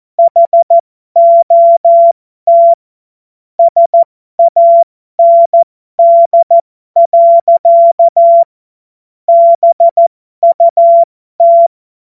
Morse code, heard as a keyed tone, about 14 wpm, spaced as for 12 wpm, 680 Hz. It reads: HOT SAND. BUT